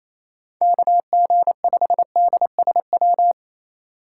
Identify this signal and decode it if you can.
Morse code, heard as a keyed tone, about 28 wpm, 700 Hz. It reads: KG5DSW
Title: USA Call Signs - 6 Characters - 3x Character Spacing - 28wpm